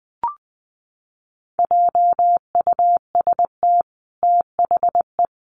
Morse code, heard as a keyed tone, about 20 wpm, 700 Hz. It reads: JUST THE
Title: Sentences from Top 1300 Words 20wpm